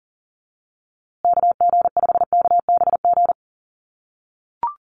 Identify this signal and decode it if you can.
Morse code, heard as a keyed tone, about 40 wpm, 700 Hz. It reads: KG5KBG
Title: USA Call Signs Repeated 3x 40wpm